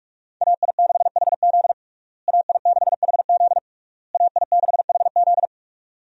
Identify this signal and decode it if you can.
Morse code, heard as a keyed tone, about 47 wpm, 700 Hz. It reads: AI6HZ AI6HZ AI6HZ
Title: USA Call Signs - 5 Characters - Repeated 3x - 45wpm